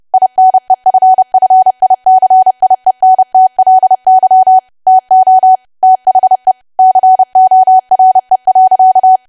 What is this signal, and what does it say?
Morse code, heard as a keyed tone, about 30 wpm, 750 Hz. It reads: INEFFICIENTLY TO THE CORE.
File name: cwa / W1AW-Level3 / 30WPM